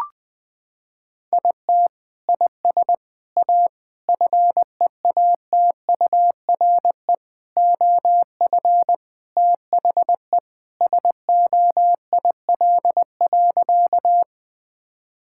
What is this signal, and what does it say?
Morse code, heard as a keyed tone, about 20 wpm, 700 Hz. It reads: IT IS A FEATURE OF THE SOIL.
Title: Sentences from Top 1000 Words - Review Part 5 - 20wpm